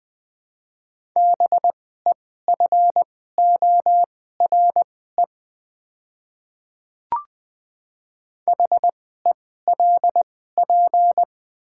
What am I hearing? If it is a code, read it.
Morse code, heard as a keyed tone, about 20 wpm, spaced as for 14 wpm, 690 Hz. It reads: BEFORE HELP
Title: 2x Character Spacing - Top 300 Words - 20wpm